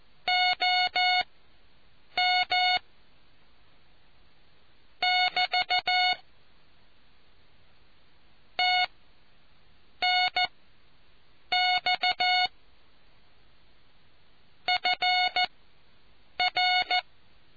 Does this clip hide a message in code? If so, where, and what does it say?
Yes: Morse code from 0.3 to 17.0 s, heard as a keyed tone, about 14 wpm, spaced as for 7 wpm, 730 Hz. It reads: OM <BT> TNX FR